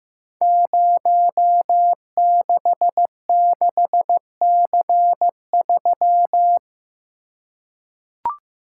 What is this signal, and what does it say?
Morse code, heard as a keyed tone, about 15 wpm, 700 Hz. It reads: 066C3
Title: Code Groups - 5 Letter:Numbers - 15wpm